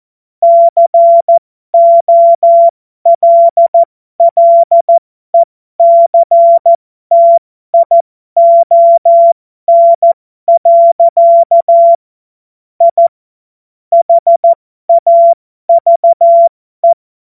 Morse code, heard as a keyed tone, about 14 wpm, spaced as for 12 wpm, 680 Hz. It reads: COLLECTION. I HAVE
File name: Morsekurs / Witze